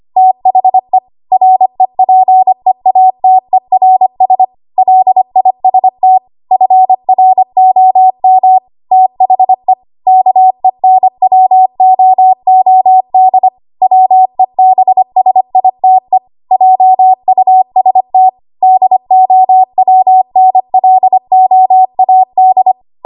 Morse code, heard as a keyed tone, about 25 wpm, 750 Hz. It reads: THE REPEATERS LIST FROM THE KENWOOD WEBSITE JUST DOWNLOAD
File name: Archive / 25 WPM / 25WPM